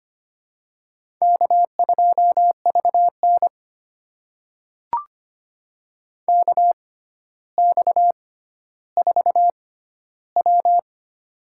Morse code, heard as a keyed tone, about 25 wpm, spaced as for 13 wpm, 700 Hz. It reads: K2VN KX4W
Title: USA Call Signs - 4 Characters - 6x Character Spacing - 25wpm